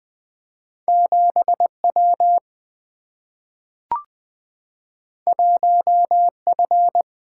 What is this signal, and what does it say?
Morse code, heard as a keyed tone, about 20 wpm, 700 Hz. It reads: 7W 1F
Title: Single-Letter Single-Number 20wpm